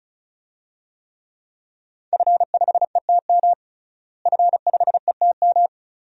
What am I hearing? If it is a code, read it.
Morse code, heard as a keyed tone, about 36 wpm, 700 Hz. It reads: F5ETM F5ETM
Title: DXCC-100 Call Signs - 5 Characters - Repeated 2x - 35wpm